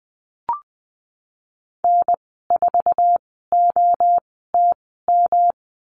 Morse code, heard as a keyed tone, about 20 wpm, spaced as for 15 wpm, 700 Hz. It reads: N4OTM
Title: 140 Lesson 23g - Letter f - Call Signs 20wpm